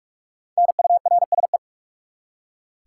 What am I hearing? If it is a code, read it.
Morse code, heard as a keyed tone, about 46 wpm, 700 Hz. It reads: NURSE
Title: Practice QSO - Exchange 3 - More Info - 45wpm